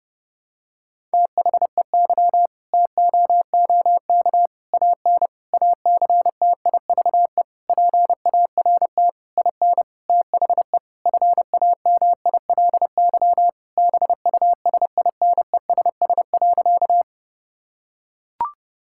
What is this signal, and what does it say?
Morse code, heard as a keyed tone, about 30 wpm, 700 Hz. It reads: THEY TOOK AN ACTIVE PART IN THE FAMILY BUSINESS.
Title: Sentences from Top 1000 Words - Review Part 6 - 30wpm